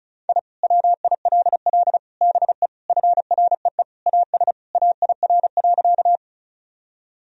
Morse code, heard as a keyed tone, about 34 wpm, 700 Hz. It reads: I WILL BE FREE AS AIR.
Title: Sentences from Top 1000 Words - Review Part 1 - 35wpm